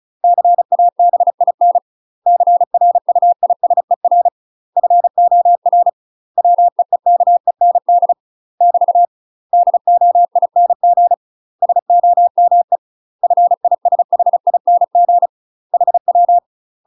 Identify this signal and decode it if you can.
Morse code, heard as a keyed tone, about 36 wpm, 700 Hz. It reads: CABIN CRUISER FOR WEEKEND <BT> DOING SOME FISHING HW
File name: UK MASTER QSOs TRAINING / 35WPM / UK MASTER QSOs 0007